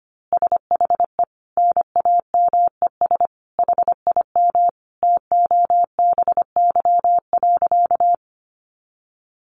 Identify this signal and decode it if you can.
Morse code, heard as a keyed tone, about 25 wpm, 700 Hz. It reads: SHE NAMES HIM TOBY.
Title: Easy Phrases 25wpm